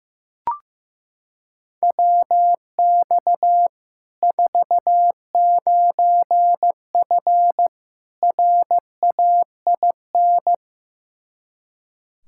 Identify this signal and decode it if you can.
Morse code, heard as a keyed tone, about 15 wpm, 700 Hz. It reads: WX 49F RAIN
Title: Practice QSO - Exchange 2 - Info - 15wpm